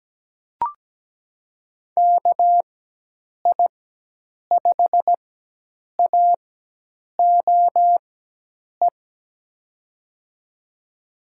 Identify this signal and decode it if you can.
Morse code, heard as a keyed tone, about 17 wpm, spaced as for 8 wpm, 700 Hz. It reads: KI5AOE
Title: USA Call Signs - 6 Characters - 4x Character Spacing - 17wpm